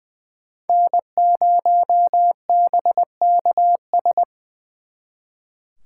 Morse code, heard as a keyed tone, about 24 wpm, 700 Hz. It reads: N0BKS
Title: Speed Racing - USA 1x3 Call Signs 20wpm